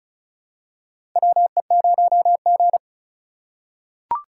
Morse code, heard as a keyed tone, about 36 wpm, 700 Hz. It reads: WE0G
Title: USA Call Signs - 4 Characters - 3x Character Spacing - 35wpm